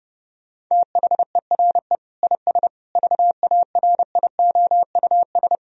Morse code, heard as a keyed tone, about 30 wpm, 700 Hz. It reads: THERE IS VARIOUS
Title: Sentences from Top 1000 Words - Review Part 4 - 30wpm